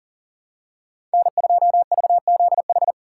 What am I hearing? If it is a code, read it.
Morse code, heard as a keyed tone, about 40 wpm, 700 Hz. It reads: N2VZH
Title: USA Call Signs - 4x Character Spacing - 40wpm